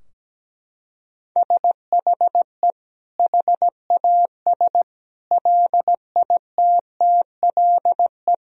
Morse code, heard as a keyed tone, about 17 wpm, 700 Hz. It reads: SHE HAS LITTLE
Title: Sentences from Top 1000 Words - Review Part 3 - 17wpm